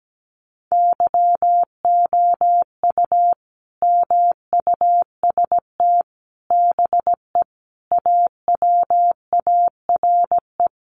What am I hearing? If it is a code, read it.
Morse code, heard as a keyed tone, about 17 wpm, 700 Hz. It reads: YOU MUST BE AWARE
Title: Sentences from Top 1200 Words 17wpm